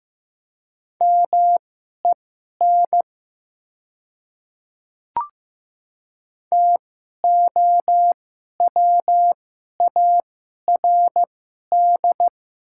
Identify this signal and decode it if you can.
Morse code, heard as a keyed tone, about 15 wpm, spaced as for 11 wpm, 700 Hz. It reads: MEN TOWARD
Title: 2x Character Spacing - Top 500 Words - 15wpm